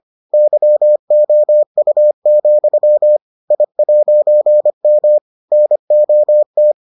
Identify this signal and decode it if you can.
Morse code, heard as a keyed tone, about 25 wpm, 600 Hz. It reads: YOU, I'M NOT